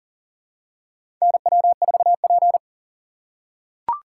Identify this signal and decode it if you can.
Morse code, heard as a keyed tone, about 40 wpm, 700 Hz. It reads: NW4P
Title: USA Call Signs - 4 Characters - 4x Character Spacing - 40wpm